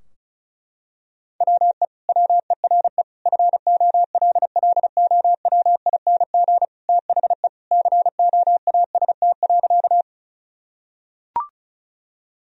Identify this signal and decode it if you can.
Morse code, heard as a keyed tone, about 36 wpm, 700 Hz. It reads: WE WERE FOLLOWING THE COAST.
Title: Sentences from Top 1000 Words - Review Part 4 - 35wpm